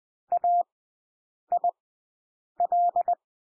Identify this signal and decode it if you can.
Morse code, heard as a keyed tone, about 20 wpm, spaced as for 8 wpm, 700 Hz. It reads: AIL
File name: cwmp3 / W1 SOUND WORDS 20 wpm